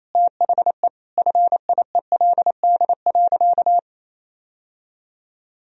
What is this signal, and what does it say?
Morse code, heard as a keyed tone, about 28 wpm, 700 Hz. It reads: THE FIELD.
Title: Sentences from Top 1000 Words - Review Part 2 - 28wpm